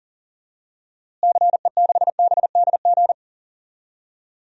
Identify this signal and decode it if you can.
Morse code, heard as a keyed tone, about 40 wpm, 700 Hz. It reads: CE6BDG
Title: DXCC-100 Call Signs - 6 Characters - Repeated 3x - 40wpm